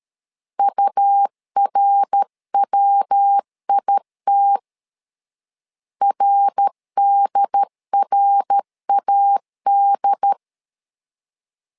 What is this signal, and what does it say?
Morse code, heard as a keyed tone, about 13 wpm, 800 Hz. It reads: URWIT RDRAD
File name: LEZIONI / 3 MWRO / 3 MWR 12wpm 1711